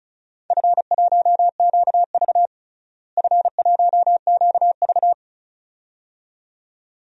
Morse code, heard as a keyed tone, about 36 wpm, 700 Hz. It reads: F1QV F1QV
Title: DXCC-100 Call Signs - 4 Characters - Repeated 2x - 35wpm